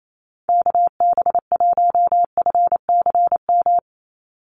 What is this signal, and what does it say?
Morse code, heard as a keyed tone, about 28 wpm, 700 Hz. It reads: KB1FCM
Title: USA Call Signs 28wpm